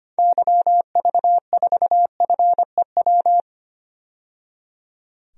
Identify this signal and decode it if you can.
Morse code, heard as a keyed tone, about 25 wpm, 700 Hz. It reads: YV4FEW
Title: DXCC-100 Call Signs - 6 Characters - 25wpm